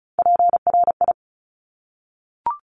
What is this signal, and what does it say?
Morse code, heard as a keyed tone, about 36 wpm, 700 Hz. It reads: PRI
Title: Top Three Letter Combinations in Words 35wpm